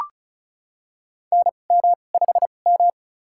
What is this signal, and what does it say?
Morse code, heard as a keyed tone, about 36 wpm, spaced as for 25 wpm, 700 Hz. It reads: NM5M
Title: USA Call Signs - 4 Characters - 2x Character Spacing - 35wpm